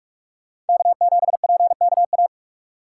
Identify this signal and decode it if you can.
Morse code, heard as a keyed tone, about 57 wpm, 700 Hz. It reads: K7PKA